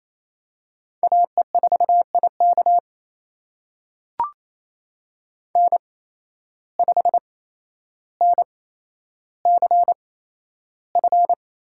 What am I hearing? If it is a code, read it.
Morse code, heard as a keyed tone, about 28 wpm, spaced as for 12 wpm, 700 Hz. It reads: AE4IK N5NCF
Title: USA Call Signs - 5 Characters - 8x Character Spacing - 28wpm